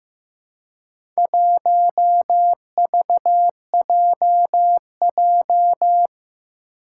Morse code, heard as a keyed tone, about 15 wpm, 700 Hz. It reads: 1VJJ
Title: Four Challenging Letter-Number Combinations 15wpm